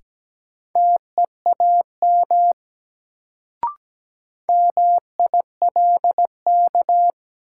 Morse code, heard as a keyed tone, about 17 wpm, 700 Hz. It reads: TEAM MILK
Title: Sentences from Top 1500 Words - New Word Drill Alt - 17wpm